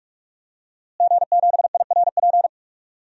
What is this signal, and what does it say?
Morse code, heard as a keyed tone, about 42 wpm, 700 Hz. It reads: G7IRP
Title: DXCC-100 Call Signs - 5 Characters - Repeated 3x - 45wpm